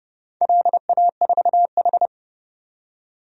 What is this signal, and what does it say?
Morse code, heard as a keyed tone, about 30 wpm, 700 Hz. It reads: LA4H
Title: DXCC-100 Call Signs - 4 Characters - 30wpm